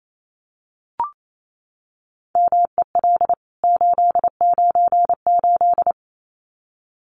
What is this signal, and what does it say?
Morse code, heard as a keyed tone, about 28 wpm, 700 Hz. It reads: MEL 898
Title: CWT Exchange 28wpm